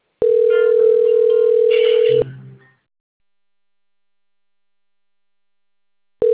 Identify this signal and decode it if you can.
Ringback tone (North America), 440 + 480 Hz.